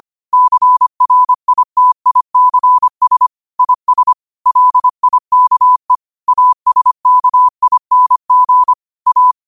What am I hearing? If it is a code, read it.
Morse code, heard as a keyed tone, about 25 wpm, 1000 Hz. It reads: CRITICS IS LIKE ASKING A
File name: qotdfiles / 25wpm